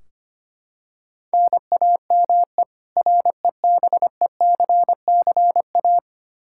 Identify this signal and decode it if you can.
Morse code, heard as a keyed tone, about 25 wpm, 700 Hz. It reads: NAME REBECCA
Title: K1USN SST QSO Practice 25wpm